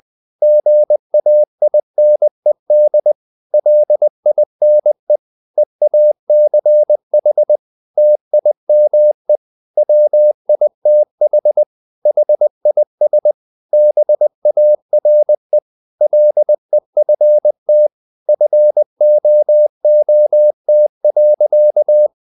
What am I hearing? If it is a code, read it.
Morse code, heard as a keyed tone, about 20 wpm, 600 Hz. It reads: GAINED LINE EACH TIME WITH HIS BARE LEFT FOOT.
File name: the old man and the sea part 2 20wpm